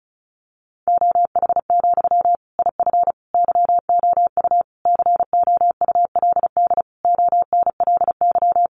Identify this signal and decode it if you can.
Morse code, heard as a keyed tone, about 34 wpm, 700 Hz. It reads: OH, IF YOU COULD ONLY
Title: Sentences from Top 1100 Words 35wpm